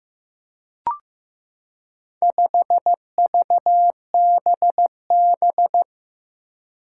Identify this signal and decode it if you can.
Morse code, heard as a keyed tone, about 15 wpm, 700 Hz. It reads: 5VBB
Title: Four Challenging Letter-Number Combinations 15wpm